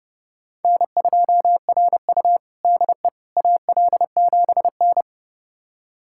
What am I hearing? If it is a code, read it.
Morse code, heard as a keyed tone, about 30 wpm, 700 Hz. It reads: N2RU DE AL7N
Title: USA Callsign DE Callsign - 4 Characters - 30wpm